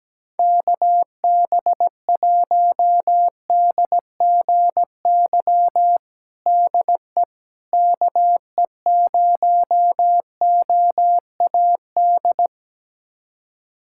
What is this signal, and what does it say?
Morse code, heard as a keyed tone, about 17 wpm, 700 Hz. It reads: KB1DGY DE KE0OAD
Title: USA Callsign DE Callsign - 6 Characters - 17wpm